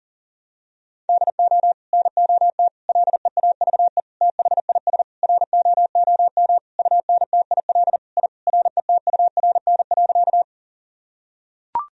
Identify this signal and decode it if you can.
Morse code, heard as a keyed tone, about 40 wpm, 700 Hz. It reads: DO NOT LEAVE THIS ROOM UNTIL I RETURN.
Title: Sentences from Top 1000 Words - Review Part 1 - 40wpm